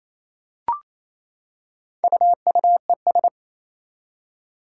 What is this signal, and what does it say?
Morse code, heard as a keyed tone, about 27 wpm, 700 Hz. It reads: UUES
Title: Four Challenging Letter-Number Combinations 28wpm